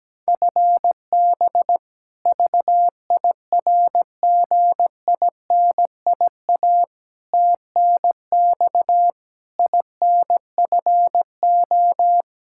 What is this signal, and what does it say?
Morse code, heard as a keyed tone, about 17 wpm, 700 Hz. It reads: FB VIRGINIA TNX INFO
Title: K1USN SST QSO Practice 17wpm